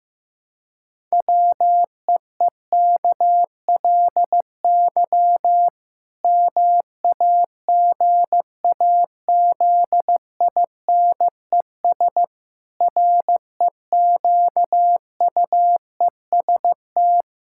Morse code, heard as a keyed tone, about 16 wpm, 700 Hz. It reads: WEEKLY MAGAZINES REQUEST
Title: Pangram Challenge v2 - 15wpm